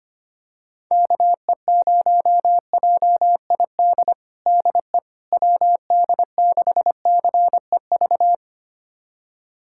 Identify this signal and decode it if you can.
Morse code, heard as a keyed tone, about 25 wpm, 700 Hz. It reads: KE0JID DE WD6CEV
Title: USA Callsign DE Callsign - 6 Characters - 25wpm